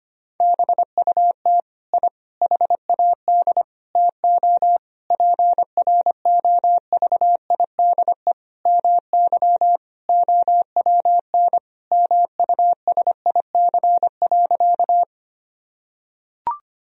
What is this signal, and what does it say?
Morse code, heard as a keyed tone, about 25 wpm, 700 Hz. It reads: BUT I HAD TO PROVIDE MY OWN MUSIC.
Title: Sentences from Top 1000 Words - Review Part 3 - 25wpm